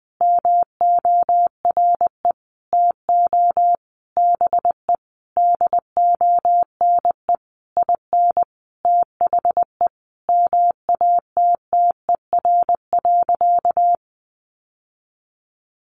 Morse code, heard as a keyed tone, about 20 wpm, 700 Hz. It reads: MORE TO BE DONE IN THE MATTER.
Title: Sentences from Top 700 Words 20wpm